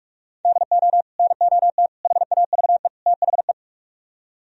Morse code, heard as a keyed tone, about 62 wpm, spaced as for 53 wpm, 700 Hz. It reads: DO NOT HAVE THE
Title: Speed-Racing - Sets of 4 Words 45wpm